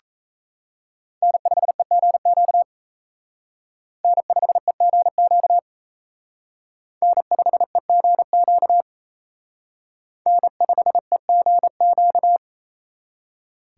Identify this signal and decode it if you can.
Morse code, heard as a keyed tone, about 35 wpm, 700 Hz. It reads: N5EGQ N5EGQ N5EGQ N5EGQ
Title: Speed Racing - USA 1x3 Call Signs 28wpm